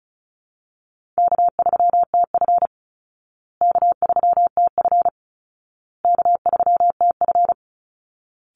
Morse code, heard as a keyed tone, about 34 wpm, 700 Hz. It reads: K3TF K3TF K3TF
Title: Call Signs Repeated 2x - 35wpm